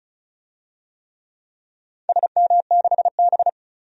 Morse code, heard as a keyed tone, about 34 wpm, 700 Hz. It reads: SM6B
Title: DXCC-100 Call Signs - 4 Characters - 35wpm